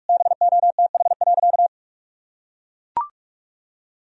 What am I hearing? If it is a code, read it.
Morse code, heard as a keyed tone, about 42 wpm, 700 Hz. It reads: BOTH.
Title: Sentences from Top 1000 Words - Review Part 1 - 45wpm